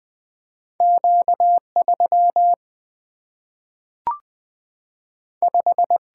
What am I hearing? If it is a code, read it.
Morse code, heard as a keyed tone, about 20 wpm, 700 Hz. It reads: Q 3 5
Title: Single-Letter Single-Number 20wpm